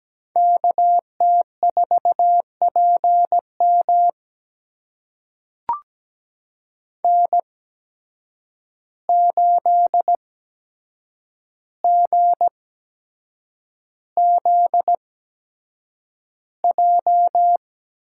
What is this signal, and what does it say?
Morse code, heard as a keyed tone, about 17 wpm, spaced as for 7 wpm, 700 Hz. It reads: KT4PM N8GZJ
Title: USA Call Signs - 5 Characters - 8x Character Spacing - 17wpm